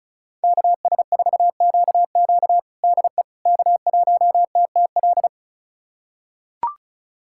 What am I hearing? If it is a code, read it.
Morse code, heard as a keyed tone, about 36 wpm, 700 Hz. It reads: KS4QQ DE K1TTL
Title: USA Callsign DE Callsign - 5 Characters - 35wpm